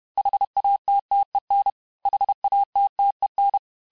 Morse code, heard as a keyed tone, about 31 wpm, 800 Hz. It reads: HATTEN HATTEN
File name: cours-du-mercredi / audios / cours-861-texte-les-mots-vitesse-29MPM